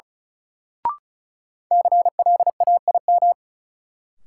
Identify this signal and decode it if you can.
Morse code, heard as a keyed tone, about 36 wpm, 700 Hz. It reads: CLAIM
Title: Sentences from Top 1500 Words - New Word Drill - 35wpm